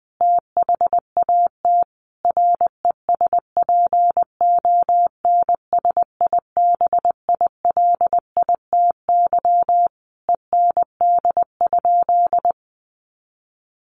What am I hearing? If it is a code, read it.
Morse code, heard as a keyed tone, about 20 wpm, 700 Hz. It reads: THAT RESPONSIBILITY END?
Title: Sentences from Top 1200 Words 20wpm